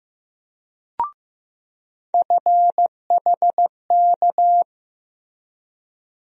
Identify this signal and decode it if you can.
Morse code, heard as a keyed tone, about 15 wpm, 700 Hz. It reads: FHK
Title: USA Call Sign Postfixes - 3 Letters 15wpm